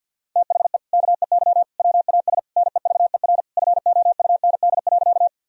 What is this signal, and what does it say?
Morse code, heard as a keyed tone, about 50 wpm, 700 Hz. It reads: THE KEY WAS NEVER FOUND.
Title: Sentences from Top 1000 Words - Review Part 4 - 50wpm